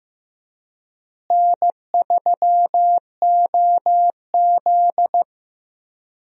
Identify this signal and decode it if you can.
Morse code, heard as a keyed tone, about 15 wpm, 700 Hz. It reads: N3OZ
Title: USA Call Signs - 4 Characters - Repeated 2x - 15wpm